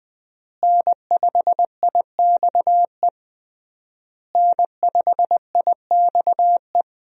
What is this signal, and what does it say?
Morse code, heard as a keyed tone, about 20 wpm, 700 Hz. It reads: N5IXE N5IXE
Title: USA Call Signs - 5 Characters - Repeated 3x - 20wpm